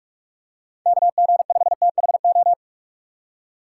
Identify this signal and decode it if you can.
Morse code, heard as a keyed tone, about 44 wpm, 700 Hz. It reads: KG5THO